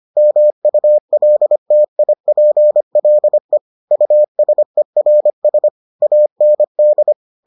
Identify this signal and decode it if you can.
Morse code, heard as a keyed tone, about 25 wpm, 600 Hz. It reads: MULTIPLE USERS AND